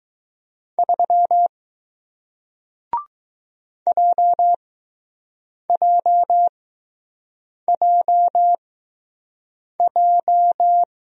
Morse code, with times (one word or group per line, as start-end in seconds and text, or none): 0.8-1.5 3
3.9-10.8 JJJJ